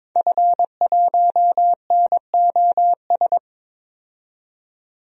Morse code, heard as a keyed tone, about 22 wpm, 700 Hz. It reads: F1NOS
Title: DXCC-100 Call Signs - 5 Characters - 22wpm